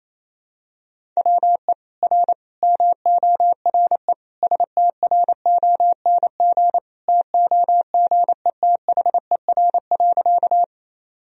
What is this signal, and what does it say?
Morse code, heard as a keyed tone, about 28 wpm, 700 Hz. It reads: WE R MORE STRONG TOGETHER.
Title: Sentences from Top 500 Words and Abbreviations 28wpm